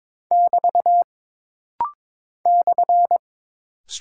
Morse code, heard as a keyed tone, about 22 wpm, 700 Hz. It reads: <BT> /
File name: practice / 22wpm / Single Punctuation-Prosign - Rapid-Fire - 22wpm